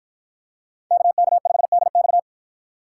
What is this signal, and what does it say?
Morse code, heard as a keyed tone, about 52 wpm, 700 Hz. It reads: KK5DX